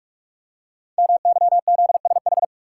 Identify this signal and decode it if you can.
Morse code, heard as a keyed tone, about 44 wpm, 700 Hz. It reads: MYZSH
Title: Code Groups - 5 Letter:Numbers - 45wpm